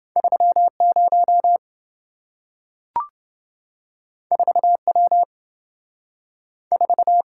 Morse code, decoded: 30 4W 4

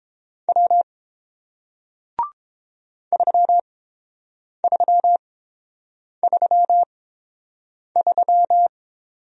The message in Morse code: W 3333

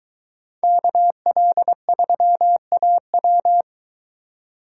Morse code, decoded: KL3AW

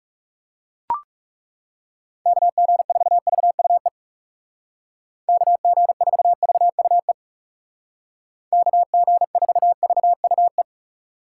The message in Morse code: KG4VUE KG4VUE KG4VUE

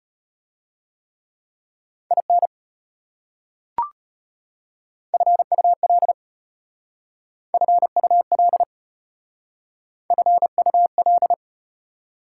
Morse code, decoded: IN FUL FUL FUL

